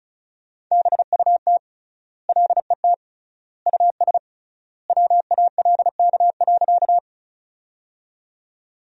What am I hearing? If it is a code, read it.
Morse code, heard as a keyed tone, about 36 wpm, 700 Hz. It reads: BUT LET US WALK.
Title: 3x Word-Spacing - Sentences from Top 300 Words - 35wpm